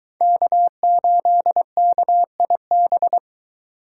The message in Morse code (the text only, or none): K8KIB